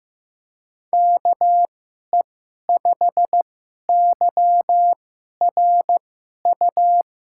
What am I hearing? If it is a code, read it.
Morse code, heard as a keyed tone, about 15 wpm, spaced as for 11 wpm, 700 Hz. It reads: KE5YRU
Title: USA Call Signs - 6 Characters - 2x Character Spacing - 15wpm